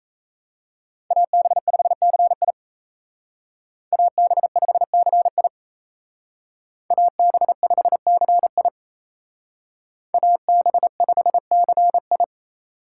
Morse code, heard as a keyed tone, about 35 wpm, 700 Hz. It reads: AB5CI AB5CI AB5CI AB5CI
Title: Speed Racing - USA 2x2 Call Signs 28wpm